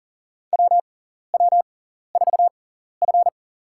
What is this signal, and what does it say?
Morse code, heard as a keyed tone, about 39 wpm, spaced as for 14 wpm, 700 Hz. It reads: WW4F
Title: USA Call Signs - 4 Characters - 6x Character Spacing - 40wpm